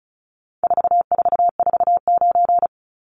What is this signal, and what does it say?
Morse code, heard as a keyed tone, about 34 wpm, 700 Hz. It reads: 4449
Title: Numbers - 4-Digit 35wpm